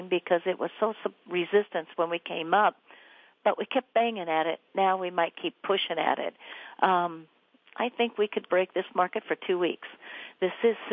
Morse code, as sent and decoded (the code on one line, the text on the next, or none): none